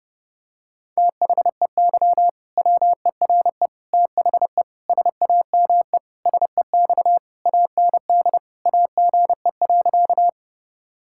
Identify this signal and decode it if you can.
Morse code, heard as a keyed tone, about 30 wpm, 700 Hz. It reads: THEY WERE THE SAME SEX AND AGE.
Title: Sentences from Top 1600 Words 30wpm